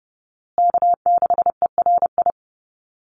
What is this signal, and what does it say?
Morse code, heard as a keyed tone, about 30 wpm, 700 Hz. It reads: K6ERI
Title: USA Call Signs Repeated 3x 30wpm